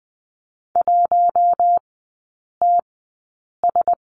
Morse code, heard as a keyed tone, about 20 wpm, spaced as for 8 wpm, 700 Hz. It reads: 1TS
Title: 041 Lesson 8e - Number 1 - Three Characters 20wpm